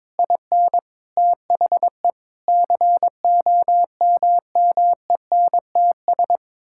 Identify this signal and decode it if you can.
Morse code, heard as a keyed tone, about 22 wpm, 700 Hz. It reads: IN THE COMMENTS
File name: practice / 22wpm / Sets of 8 Words - 22wpm